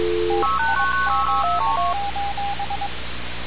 Morse code, heard as a keyed tone, about 22 wpm, 800 Hz, with radio noise. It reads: TIME:
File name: radiocap / REC z